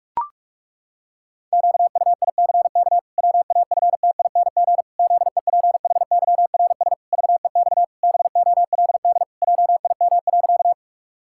Morse code, heard as a keyed tone, about 41 wpm, 700 Hz. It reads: QUICK WAFTING ZEPHYRS VEX BOLD JIM.
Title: Pangram Challenge v2 - 45wpm